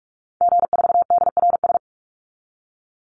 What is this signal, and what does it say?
Morse code, heard as a keyed tone, about 44 wpm, 700 Hz. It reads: G4DRS
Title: Call Signs - 45wpm